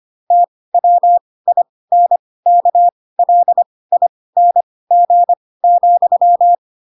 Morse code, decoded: TWINKLING,

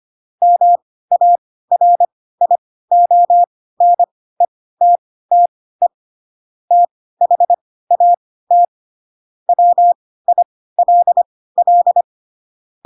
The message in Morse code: MARIONETTE THAT WILL

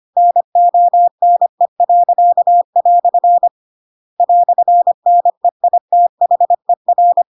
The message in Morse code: NONE." "NEITHER

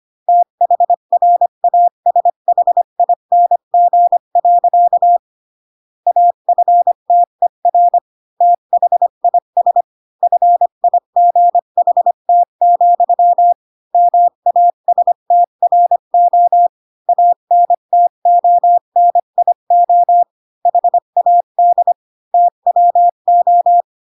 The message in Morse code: THRASHING. AFTER THIS FIGHT, MASTRO ANTONIO HAD TWO

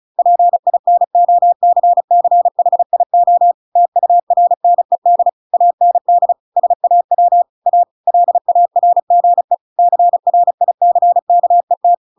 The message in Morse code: PINOCCHIO TURNED AND SAW A LARGE CRICKET

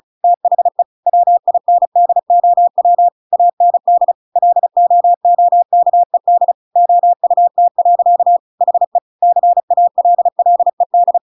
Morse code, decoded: THE WINDOW AND LOOKED OUT. HE CALLED

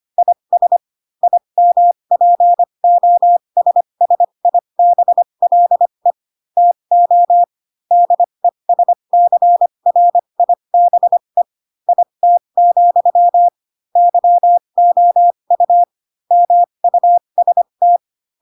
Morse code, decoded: IS IMPOSSIBLE TO DESCRIBE IT, YOU MUST

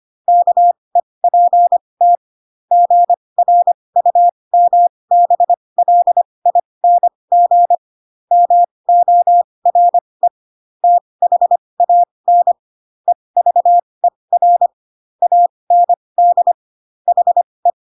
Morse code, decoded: KEPT GRUMBLING MORE THAN EVER AND HE

